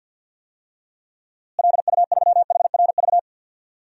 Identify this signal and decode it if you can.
Morse code, heard as a keyed tone, about 47 wpm, 700 Hz. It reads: LU3HRV
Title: DXCC-100 Call Signs - 6 Characters - Repeated 3x - 50wpm